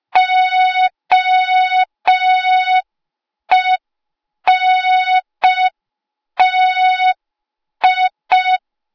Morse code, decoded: OENTI